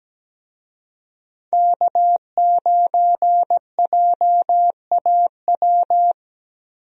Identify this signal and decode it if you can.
Morse code, heard as a keyed tone, about 17 wpm, 700 Hz. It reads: K9JAW
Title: USA Call Signs - 5 Characters - Repeated 3x - 17wpm